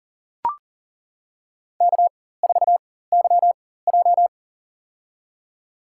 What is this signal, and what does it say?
Morse code, heard as a keyed tone, about 40 wpm, spaced as for 19 wpm, 700 Hz. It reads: K4YJ
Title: USA Call Signs - 4x Character Spacing - 40wpm